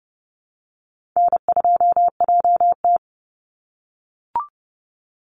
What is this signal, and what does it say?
Morse code, heard as a keyed tone, about 30 wpm, 700 Hz. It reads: N2JT
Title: Easy Call Signs - 30wpm